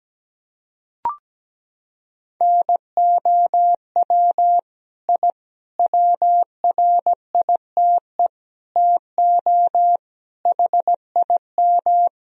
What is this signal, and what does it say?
Morse code, heard as a keyed tone, about 17 wpm, 700 Hz. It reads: NOW I WRITE TO HIM
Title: Sentences from Top 1000 Words - Review Part 1 - 17wpm